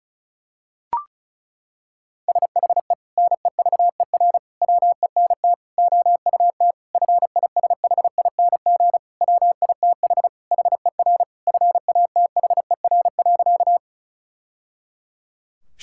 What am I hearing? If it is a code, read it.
Morse code, heard as a keyed tone, about 34 wpm, 700 Hz. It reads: SHE NEVER WENT OUT FISHING WITH HER FATHER.
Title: Sentences from Top 1000 Words - Review Part 5 - 35wpm